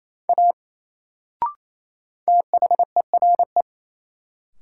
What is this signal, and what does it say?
Morse code, heard as a keyed tone, about 28 wpm, 700 Hz. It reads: A THERE
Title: Sentences from Top 100 Words - New Word Drill - 28wpm